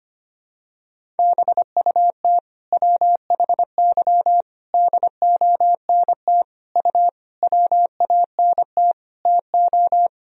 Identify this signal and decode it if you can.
Morse code, heard as a keyed tone, about 25 wpm, 700 Hz. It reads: BUT WHY DONT U WANT TO
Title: Sentences from Top 300 Words and Abbreviations 25wpm